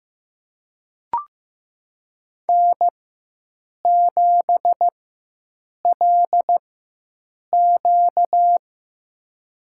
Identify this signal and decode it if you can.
Morse code, heard as a keyed tone, about 15 wpm, spaced as for 7 wpm, 700 Hz. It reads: N7LQ
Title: USA Call Signs - 4 Characters - 4x Character Spacing - 15wpm